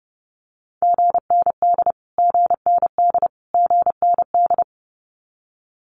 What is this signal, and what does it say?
Morse code, heard as a keyed tone, about 30 wpm, 700 Hz. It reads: GND GND GND
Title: Abbreviations Repeated 3x - 30wpm